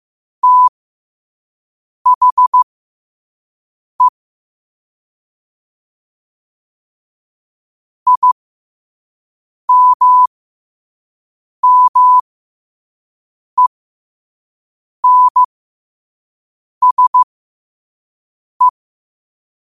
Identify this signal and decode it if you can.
Morse code, heard as a keyed tone, about 15 wpm, spaced as for 5 wpm, 1000 Hz. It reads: THE IMMENSE
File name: qotdfiles / 5wpm